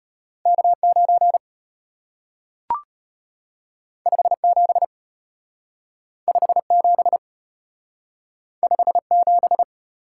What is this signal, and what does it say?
Morse code, heard as a keyed tone, about 33 wpm, 700 Hz. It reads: K9 57 57 57